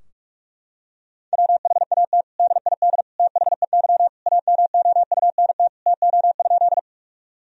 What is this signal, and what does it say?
Morse code, heard as a keyed tone, about 46 wpm, 700 Hz. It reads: WHAT DID THEY AMOUNT TO?
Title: Sentences from Top 1000 Words - Review Part 4 - 45wpm